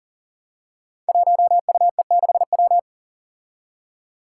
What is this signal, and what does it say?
Morse code, heard as a keyed tone, about 40 wpm, 700 Hz. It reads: 1UE6W
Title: Five Challenging Letter-Number Combinations 40wpm